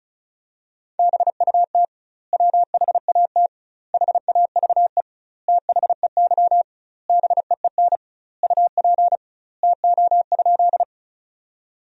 Morse code, heard as a keyed tone, about 36 wpm, 700 Hz. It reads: BUT WHAT HAVE THEY BEEN UP TO?
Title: 2x Word-Spacing - Sentences from Top 100 Words - 35wpm